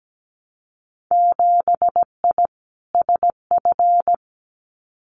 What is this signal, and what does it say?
Morse code, heard as a keyed tone, about 17 wpm, 700 Hz. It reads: 7I SF
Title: Winter Field Day Exchanges Encore 17wpm